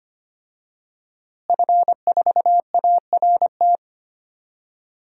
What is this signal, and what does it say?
Morse code, heard as a keyed tone, about 25 wpm, 700 Hz. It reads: F4ART
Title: DXCC-100 Call Signs - 5 Characters - Repeated 2x - 25wpm